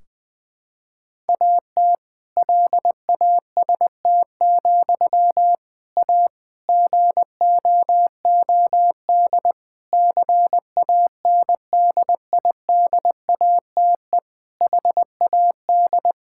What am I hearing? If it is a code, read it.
Morse code, heard as a keyed tone, about 20 wpm, 700 Hz. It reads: AT LAST, A GOOD CANDIDATE HAD
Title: Sentences from Top 1600 Words 20wpm